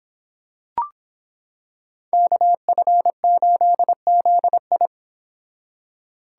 Morse code, heard as a keyed tone, about 26 wpm, 700 Hz. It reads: KF8ZI